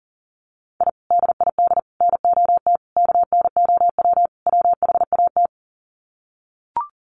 Morse code, heard as a keyed tone, about 40 wpm, 700 Hz. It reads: I DID NOT KNOW WHAT
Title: Sets of 5 Words 40wpm